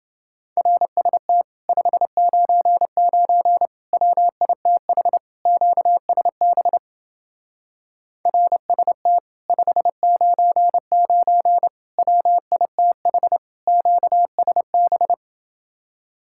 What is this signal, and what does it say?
Morse code, heard as a keyed tone, about 28 wpm, 700 Hz. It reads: RST 599 WITH QSB RST 599 WITH QSB